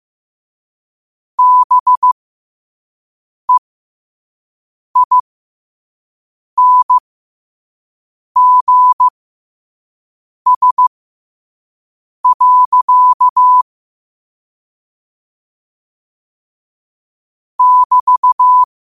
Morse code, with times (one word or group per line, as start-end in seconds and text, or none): 1.4-13.6 BEINGS.
17.6-18.6 <BT>